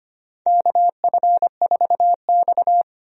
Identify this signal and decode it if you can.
Morse code, heard as a keyed tone, about 25 wpm, 700 Hz. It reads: KF4X
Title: USA Call Signs - 4 Characters - 25wpm